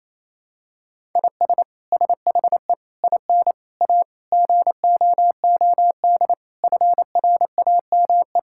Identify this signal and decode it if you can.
Morse code, heard as a keyed tone, about 28 wpm, 700 Hz. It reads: IS SHE IN A GOOD FRAME
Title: Sentences from Top 1000 Words - Review Part 6 - 28wpm